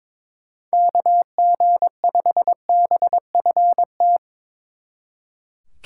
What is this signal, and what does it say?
Morse code, heard as a keyed tone, about 25 wpm, 700 Hz. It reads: KG5BFT
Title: Speed Racing - USA 2x3 Call Signs 22wpm